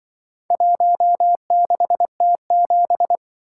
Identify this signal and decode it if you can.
Morse code, heard as a keyed tone, about 26 wpm, 700 Hz. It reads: 16T7